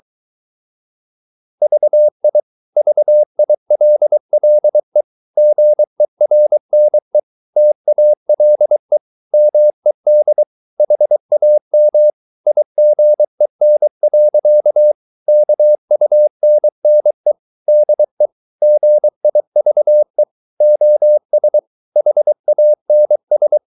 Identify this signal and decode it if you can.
Morse code, heard as a keyed tone, about 23 wpm, 600 Hz. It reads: VI VILLE GERNE TALE MED HAM IGEN. KUNNE DE GIVE OS HANS
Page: Ferie 23 wpm